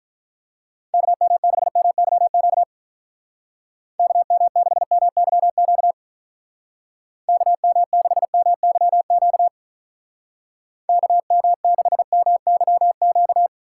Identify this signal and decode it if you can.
Morse code, heard as a keyed tone, about 44 wpm, 700 Hz. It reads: KM6MYQ KM6MYQ KM6MYQ KM6MYQ